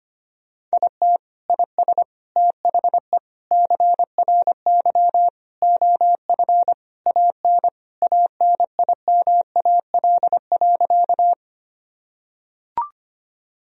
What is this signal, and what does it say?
Morse code, heard as a keyed tone, about 25 wpm, 700 Hz. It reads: IT IS THE CRY OF AN ANIMAL.
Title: Sentences from Top 1000 Words - Review Part 2 - 25wpm